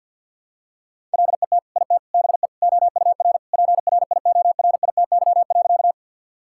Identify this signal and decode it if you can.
Morse code, heard as a keyed tone, about 50 wpm, 700 Hz. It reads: LET IT BE OUR PRIORITY.
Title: Sentences from Top 1600 Words 50wpm